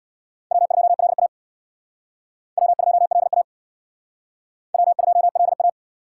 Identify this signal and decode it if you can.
Morse code, heard as a keyed tone, about 65 wpm, 700 Hz. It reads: W2LA W2LA W2LA